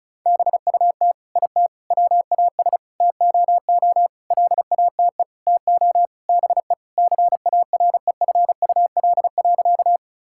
Morse code, heard as a keyed tone, about 36 wpm, 700 Hz. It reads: BUT IT WAS TOO LATE TO BE CAREFUL.
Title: Sentences from Top 1600 Words 35wpm